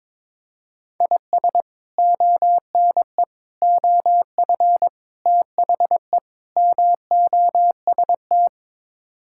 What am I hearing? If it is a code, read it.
Morse code, heard as a keyed tone, about 22 wpm, 700 Hz. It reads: IS ONE OF THE MOST
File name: practice / 22wpm / Build A Phrase - 5-Word Phrases - 22wpm